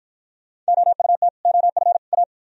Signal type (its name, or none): Morse code